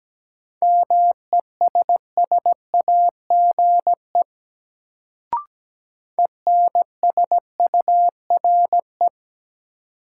Morse code, heard as a keyed tone, about 17 wpm, 700 Hz. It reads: MESSAGE ENSURE
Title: Sentences from Top 1100 Words - New Word Drill Alt - 17wpm